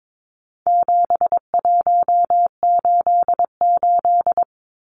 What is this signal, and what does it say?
Morse code, heard as a keyed tone, about 22 wpm, 700 Hz. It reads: 7188
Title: Numbers - 4-Digit 22wpm